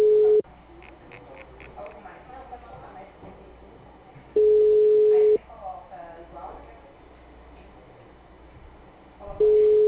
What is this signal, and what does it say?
Ringback tone (Europe, 425 Hz), with speech.
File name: out- -2019- - - .5176